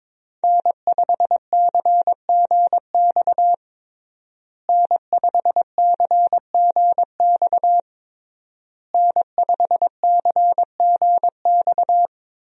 Morse code, heard as a keyed tone, about 22 wpm, 700 Hz. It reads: N5CGX N5CGX N5CGX
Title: USA Call Signs - 5 Characters - Repeated 3x - 22wpm